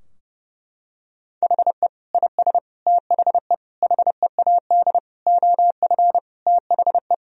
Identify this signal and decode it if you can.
Morse code, heard as a keyed tone, about 30 wpm, 700 Hz. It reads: HE IS THE HEAD OF THE
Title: Sentences from Top 1000 Words - Review Part 4 - 30wpm